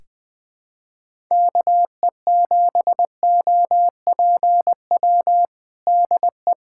Morse code, heard as a keyed tone, about 21 wpm, 700 Hz. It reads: KE7OPW DE
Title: USA Callsign DE Callsign - 6 Characters - 20wpm